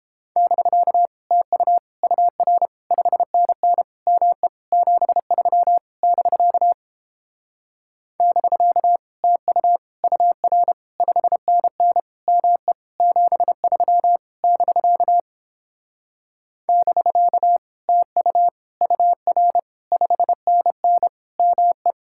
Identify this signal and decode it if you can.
Morse code, heard as a keyed tone, about 29 wpm, 700 Hz. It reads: <BK> TU UR 5NN ME 73 <BK> <BK> TU UR 5NN ME 73 <BK> <BK> TU UR 5NN ME